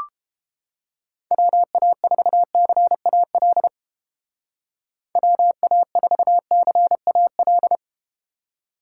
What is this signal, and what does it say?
Morse code, heard as a keyed tone, about 30 wpm, 700 Hz. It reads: WA4CAL WA4CAL